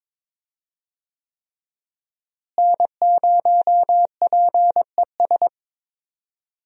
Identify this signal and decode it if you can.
Morse code, heard as a keyed tone, about 22 wpm, 700 Hz. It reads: N0PES
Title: USA Call Signs - 5 Characters - Repeated 3x - 22wpm